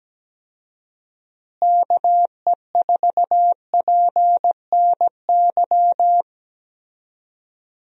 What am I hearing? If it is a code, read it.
Morse code, heard as a keyed tone, about 17 wpm, 700 Hz. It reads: KE4PNY
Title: USA Call Signs - 6 Characters - Repeated 3x - 17wpm